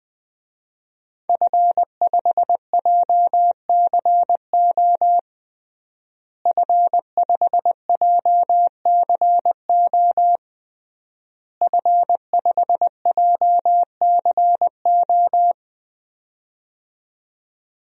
Morse code, heard as a keyed tone, about 21 wpm, 700 Hz. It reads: F5JCO F5JCO F5JCO
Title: DXCC-100 Call Signs - 5 Characters - Repeated 3x - 20wpm